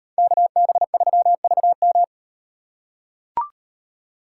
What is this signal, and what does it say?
Morse code, heard as a keyed tone, about 36 wpm, 700 Hz. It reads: KB3VM